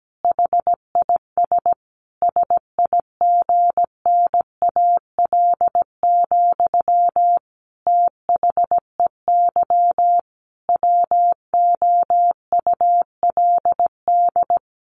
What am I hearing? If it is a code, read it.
Morse code, heard as a keyed tone, about 17 wpm, 700 Hz. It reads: HIS SIGNAL, THEY WOULD
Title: Sentences from Top 1400 Words 17wpm